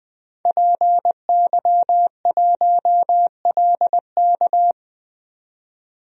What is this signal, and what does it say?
Morse code, heard as a keyed tone, about 21 wpm, 700 Hz. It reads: PY1LK
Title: DXCC-100 Call Signs - 5 Characters - 20wpm